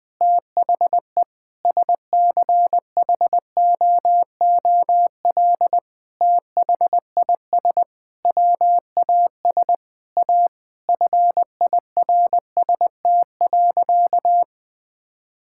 Morse code, heard as a keyed tone, about 20 wpm, 700 Hz. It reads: THE SCHOOL THIS WAS A FIRST.
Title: Sentences with 5-Word Phrases - 20wpm